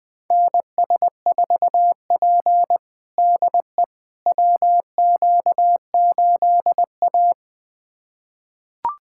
Morse code, heard as a keyed tone, about 20 wpm, 700 Hz. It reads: NS4P DE WQ8A
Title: USA Callsign DE Callsign - 4 Characters - 20wpm